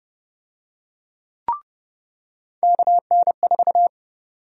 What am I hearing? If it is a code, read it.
Morse code, heard as a keyed tone, about 30 wpm, 700 Hz. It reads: KN4
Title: USA Call Sign Prefixes - 30wpm